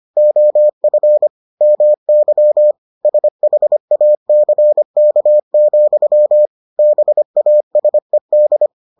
Morse code, heard as a keyed tone, about 25 wpm, 600 Hz. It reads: OF MY SHACK, BASED